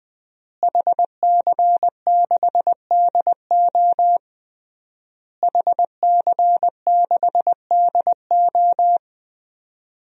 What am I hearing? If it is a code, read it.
Morse code, heard as a keyed tone, about 20 wpm, 700 Hz. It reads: HC6DO HC6DO
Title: DXCC-100 Call Signs - 5 Characters - Repeated 3x - 20wpm